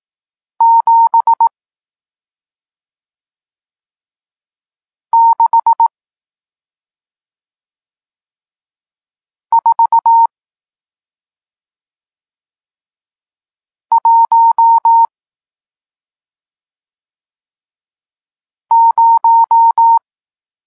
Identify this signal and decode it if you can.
Morse code, heard as a keyed tone, about 18 wpm, 920 Hz. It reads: 7 6 4 1 0
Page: Tekst 40 – Nauka znaków przestankowych